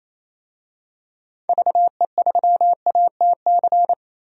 Morse code, heard as a keyed tone, about 28 wpm, 700 Hz. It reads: VE3ATC
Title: DXCC-100 Call Signs - 6 Characters - 28wpm